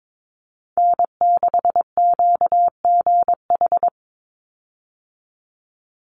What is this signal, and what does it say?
Morse code, heard as a keyed tone, about 22 wpm, 700 Hz. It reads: N6QGH
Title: USA Call Signs 22wpm